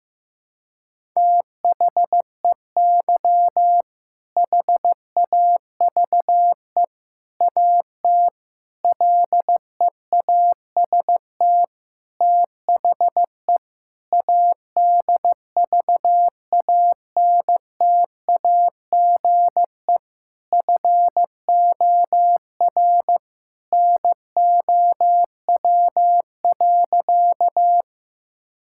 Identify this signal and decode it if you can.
Morse code, heard as a keyed tone, about 15 wpm, 700 Hz. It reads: THEY HAVE AT LEAST THE ADVANTAGE FOR NOW.
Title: Sentences from Top 1000 Words - Review Part 6 - 15wpm